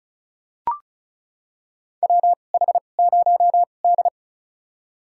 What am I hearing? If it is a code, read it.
Morse code, heard as a keyed tone, about 36 wpm, spaced as for 25 wpm, 700 Hz. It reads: WH0D
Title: USA Call Signs - 4 Characters - 2x Character Spacing - 35wpm